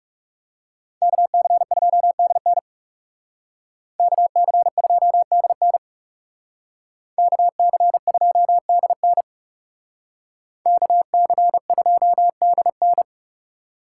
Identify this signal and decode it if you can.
Morse code, heard as a keyed tone, about 38 wpm, 700 Hz. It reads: KC2DN KC2DN KC2DN KC2DN